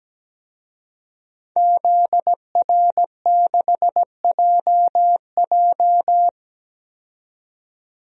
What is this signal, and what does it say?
Morse code, heard as a keyed tone, about 17 wpm, 700 Hz. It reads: ZR6JJ
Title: DXCC-100 Call Signs - 5 Characters - 17wpm